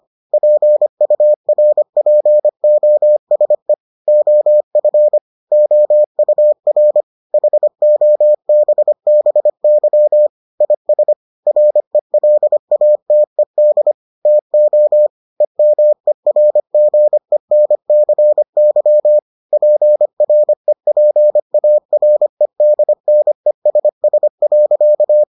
Morse code, heard as a keyed tone, about 25 wpm, 600 Hz. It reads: PURPOSE OF OUR HOBBY IS RELATED TO EMERGENCY PREPAREDNESS.